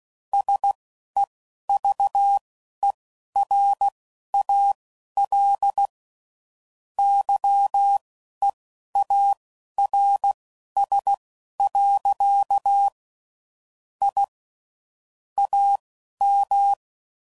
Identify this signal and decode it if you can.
Morse code, heard as a keyed tone, about 16 wpm, spaced as for 11 wpm, 800 Hz. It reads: SEVERAL YEARS. I AM
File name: CW Code Groups @10WPM / QSOs / QSO kc7sac